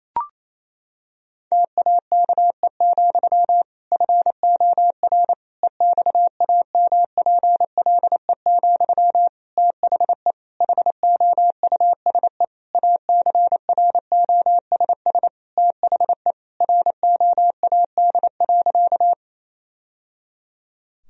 Morse code, heard as a keyed tone, about 28 wpm, 700 Hz. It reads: TAKE, FOR EXAMPLE, THE HOUSE ACROSS THE ROAD.
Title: Sentences from Top 1000 Words - Review Part 1 - 28wpm